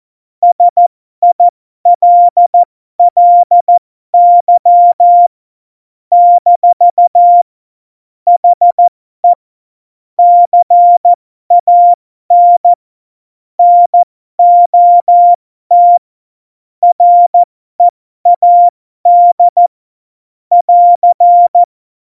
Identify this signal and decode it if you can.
Morse code, heard as a keyed tone, about 14 wpm, spaced as for 12 wpm, 700 Hz. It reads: SILLY - HE CAN NOT READ <AR>
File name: Morsekurs / Witze